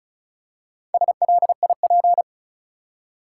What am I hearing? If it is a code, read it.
Morse code, heard as a keyed tone, about 34 wpm, 700 Hz. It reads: SLIP
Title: Four Letter Words - 35wpm